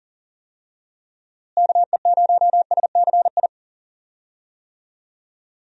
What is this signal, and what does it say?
Morse code, heard as a keyed tone, about 40 wpm, 700 Hz. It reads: KE0SCI
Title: USA Call Signs - 6 Characters - Repeated 2x - 40wpm